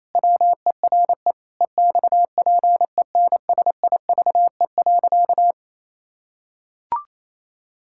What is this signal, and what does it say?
Morse code, heard as a keyed tone, about 27 wpm, 700 Hz. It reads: WERE EXPENSIVE.
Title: Pangram Challenge v2 - 28wpm